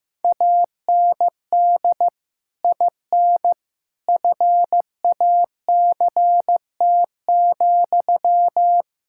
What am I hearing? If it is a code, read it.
Morse code, heard as a keyed tone, about 15 wpm, 700 Hz. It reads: AND IN FACT,
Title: Sentences from Top 1600 Words 15wpm